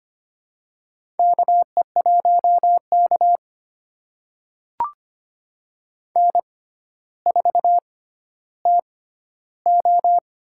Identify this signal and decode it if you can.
Morse code, heard as a keyed tone, about 25 wpm, spaced as for 13 wpm, 700 Hz. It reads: KE1K N4TO
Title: USA Call Signs - 4 Characters - 6x Character Spacing - 25wpm